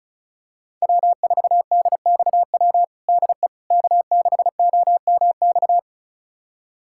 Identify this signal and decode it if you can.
Morse code, heard as a keyed tone, about 36 wpm, 700 Hz. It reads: W4DXW DE K6OMX
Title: USA Callsign DE Callsign - 5 Characters - 35wpm